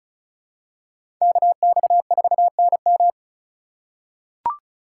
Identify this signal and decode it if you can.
Morse code, heard as a keyed tone, about 36 wpm, 700 Hz. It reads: KX4NM
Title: K1USN SST QSO Practice - 35wpm